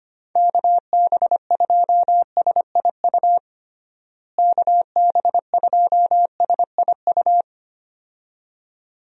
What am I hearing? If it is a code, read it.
Morse code, heard as a keyed tone, about 25 wpm, 700 Hz. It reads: KB2SIU KB2SIU
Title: USA Call Signs - 6 Characters - Repeated 2x - 25wpm